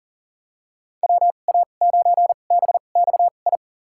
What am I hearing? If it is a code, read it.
Morse code, heard as a keyed tone, about 38 wpm, spaced as for 28 wpm, 700 Hz. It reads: WA9BXI
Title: USA Call Signs - 6 Characters - 2x Character Spacing - 40wpm